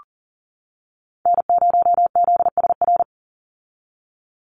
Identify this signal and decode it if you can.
Morse code, heard as a keyed tone, about 39 wpm, 700 Hz. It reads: N0ZSR
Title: USA Call Signs 40wpm